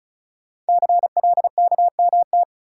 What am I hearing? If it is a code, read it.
Morse code, heard as a keyed tone, about 36 wpm, 690 Hz. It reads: CLKMT
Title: Code Groups - 5 Letters - 35wpm